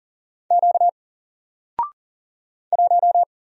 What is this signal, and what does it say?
Morse code, heard as a keyed tone, about 40 wpm, 700 Hz. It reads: Q 1
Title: Single Character (Letter:Number:Punctuation:Prosign) Alt 40wpm